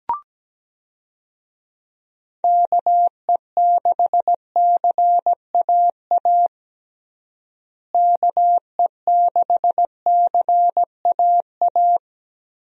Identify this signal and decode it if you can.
Morse code, heard as a keyed tone, about 17 wpm, 700 Hz. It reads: KE6CAA KE6CAA
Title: USA Call Signs - 6 Characters - Repeated 3x - 17wpm